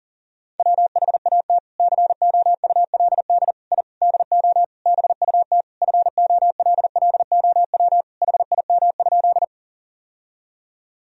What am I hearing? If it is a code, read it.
Morse code, heard as a keyed tone, about 40 wpm, 700 Hz. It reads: WHAT COULD I DO BUT FOLLOW HIM?
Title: Sentences from Top 1000 Words - Review Part 1 - 40wpm